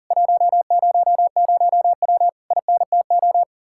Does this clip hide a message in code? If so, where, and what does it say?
Yes: Morse code from 0.1 to 3.4 s, heard as a keyed tone, about 52 wpm, 700 Hz. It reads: 100W INTO